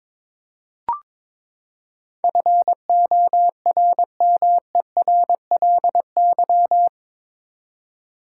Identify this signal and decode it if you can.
Morse code, heard as a keyed tone, about 22 wpm, 700 Hz. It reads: FORMERLY
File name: practice / 22wpm / Eight Letter Words - 22wpm